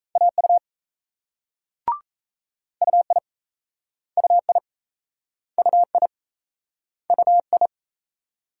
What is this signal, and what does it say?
Morse code, heard as a keyed tone, about 38 wpm, 700 Hz. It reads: AU UI UI UI UI